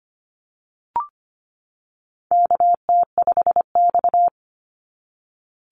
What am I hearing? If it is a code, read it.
Morse code, heard as a keyed tone, about 25 wpm, 700 Hz. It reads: KT5X
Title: NA SOTA Activator Callsigns 25wpm